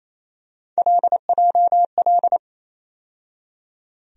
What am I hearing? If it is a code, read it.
Morse code, heard as a keyed tone, about 28 wpm, 700 Hz. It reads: LJL
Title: USA Call Sign Postfixes - 3 Letters 28wpm